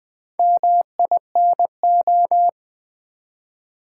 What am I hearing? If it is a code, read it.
Morse code, heard as a keyed tone, about 20 wpm, 700 Hz. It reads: MINO
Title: Build A Word - 4 Letters At A Time - 8-Letter Words - 20wpm